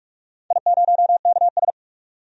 Morse code, heard as a keyed tone, about 46 wpm, 700 Hz. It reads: I0KS
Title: DXCC-100 Call Signs - 4 Characters - 45wpm